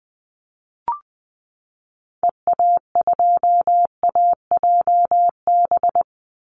Morse code, heard as a keyed tone, about 20 wpm, 700 Hz. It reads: EA2AJB
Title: Call Signs Repeated 2x - 20wpm